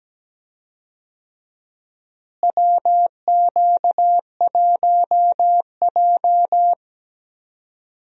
Morse code, heard as a keyed tone, about 17 wpm, 700 Hz. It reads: WQ1J
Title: USA Call Signs - 4 Characters - Repeated 3x - 17wpm